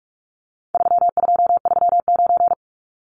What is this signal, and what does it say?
Morse code, heard as a keyed tone, about 44 wpm, 700 Hz. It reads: 3239
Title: Numbers - 4-Digit 45wpm